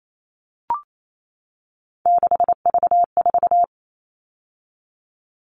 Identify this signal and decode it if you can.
Morse code, heard as a keyed tone, about 28 wpm, 700 Hz. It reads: 6V4
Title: All Cut Numbers - 3-Digit 28wpm